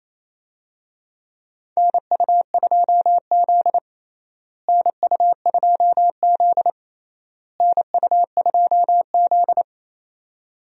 Morse code, heard as a keyed tone, about 28 wpm, 700 Hz. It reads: NU2Z NU2Z NU2Z
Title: USA Call Signs - 4 Characters - Repeated 3x - 28wpm